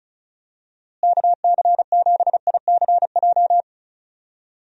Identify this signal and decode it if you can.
Morse code, heard as a keyed tone, about 34 wpm, 700 Hz. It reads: KC7ICJ
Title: USA Call Signs - 6 Characters - Repeated 3x - 35wpm